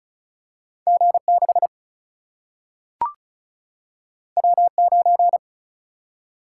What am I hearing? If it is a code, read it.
Morse code, heard as a keyed tone, about 36 wpm, 700 Hz. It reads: G6 W9
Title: Single-Letter Single-Number 35wpm